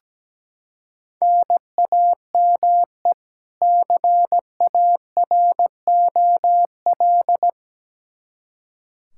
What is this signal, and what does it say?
Morse code, heard as a keyed tone, about 17 wpm, 700 Hz. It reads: NAME CAROL
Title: Practice QSO - Exchange 1 - Beginning - 17wpm